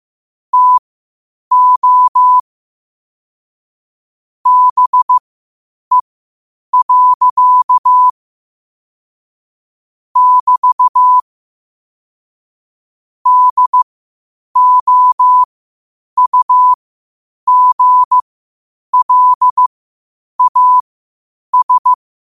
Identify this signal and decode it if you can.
Morse code, heard as a keyed tone, about 15 wpm, spaced as for 8 wpm, 1000 Hz. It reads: TO BE. <BT> DOUGLAS
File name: qotdfiles / 8wpm